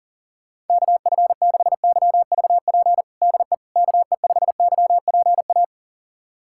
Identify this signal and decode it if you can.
Morse code, heard as a keyed tone, about 40 wpm, 700 Hz. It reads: KF6YVP DE KE5YPA
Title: USA Callsign DE Callsign 40wpm